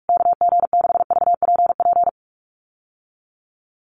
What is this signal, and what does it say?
Morse code, heard as a keyed tone, about 44 wpm, 690 Hz. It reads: KG6VPP
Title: USA Call Signs Repeated 3x 45wpm